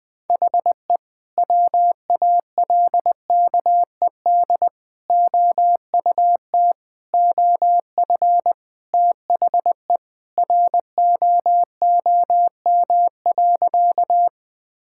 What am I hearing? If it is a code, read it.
Morse code, heard as a keyed tone, about 20 wpm, 700 Hz. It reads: HE WALKED OUT OF THE ROOM.
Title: Sentences with 2-Word Phrases - Spoken Beginning - 20wpm